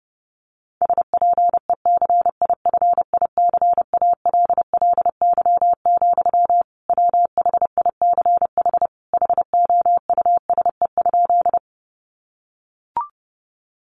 Morse code, heard as a keyed tone, about 30 wpm, 700 Hz. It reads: SPECIFICALLY, WHICH HOUSE?
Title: Sentences from Top 1100 Words 30wpm